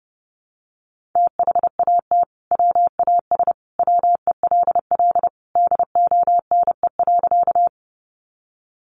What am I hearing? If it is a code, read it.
Morse code, heard as a keyed tone, about 30 wpm, 700 Hz. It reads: THAT WAS WELL DONE.
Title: Sentences from Top 500 Words 30wpm